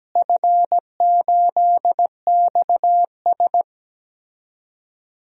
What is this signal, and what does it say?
Morse code, heard as a keyed tone, about 17 wpm, 700 Hz. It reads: F8XS
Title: DXCC-100 Call Signs - 4 Characters - Repeated 3x - 17wpm